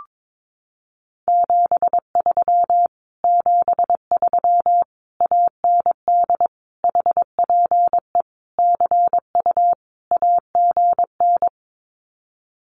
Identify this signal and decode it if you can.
Morse code, heard as a keyed tone, about 22 wpm, 700 Hz. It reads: 73 73 AND HPE CU AGN
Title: Common QSO Phrases Encore 22wpm